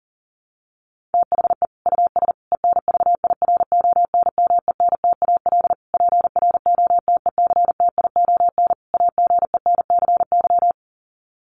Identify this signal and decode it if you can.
Morse code, heard as a keyed tone, about 37 wpm, 700 Hz. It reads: THE US ENVIRONMENTAL PROTECTION AGENCY
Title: Sets of 5 Words Encore2 40wpm